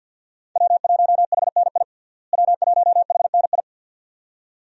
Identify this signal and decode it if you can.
Morse code, heard as a keyed tone, about 51 wpm, 700 Hz. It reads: W1HNI W1HNI
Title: USA Call Signs - 5 Characters - Repeated 2x - 50wpm